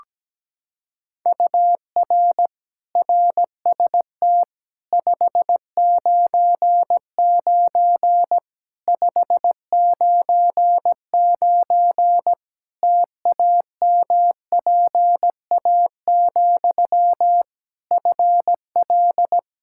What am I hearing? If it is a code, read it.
Morse code, heard as a keyed tone, about 17 wpm, 700 Hz. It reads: UR RST 599 599 TAMPA, FL
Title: K1USN SST QSO Practice 17wpm